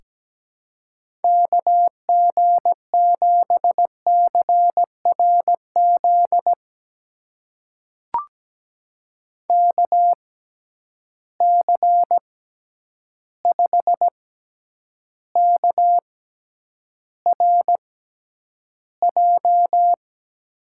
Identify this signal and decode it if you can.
Morse code, heard as a keyed tone, about 17 wpm, spaced as for 9 wpm, 700 Hz. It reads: KG7CRZ KC5KRJ
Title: USA Call Signs - 6 Characters - 6x Character Spacing - 17wpm